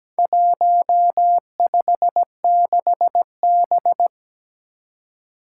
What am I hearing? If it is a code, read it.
Morse code, heard as a keyed tone, about 17 wpm, 700 Hz. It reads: 156B
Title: Four Challenging Letter-Number Combinations 17wpm